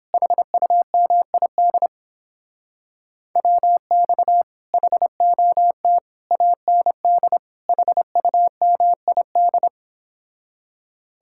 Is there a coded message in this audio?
Yes: Morse code from 0.1 to 9.7 s, heard as a keyed tone, about 29 wpm, 700 Hz. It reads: HUMID WX HOT AND HUMID